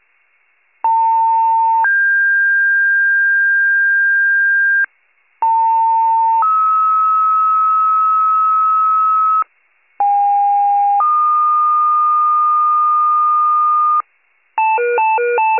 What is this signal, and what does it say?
Two-tone page, heard as a one-second tone, then a three-second tone, about 900 then 1600 Hz, about 900 then 1290 Hz, about 810 then 1220 Hz. It is heard from 0.8 to 14.0 s.
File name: wcalert